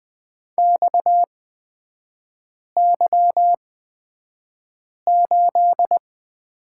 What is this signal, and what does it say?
Morse code, heard as a keyed tone, about 20 wpm, spaced as for 5 wpm, 700 Hz. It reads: XY8
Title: 250 Lesson 40e - BK - Three Characters 20wpm